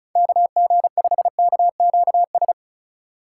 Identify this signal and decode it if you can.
Morse code, heard as a keyed tone, about 36 wpm, 700 Hz. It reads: KG5KQS
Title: USA Call Signs - 2x Character Spacing - 35wpm